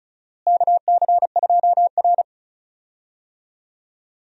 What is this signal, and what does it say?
Morse code, heard as a keyed tone, about 36 wpm, 700 Hz. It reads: KC2R
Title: USA Call Signs - 4 Characters - Repeated 2x - 35wpm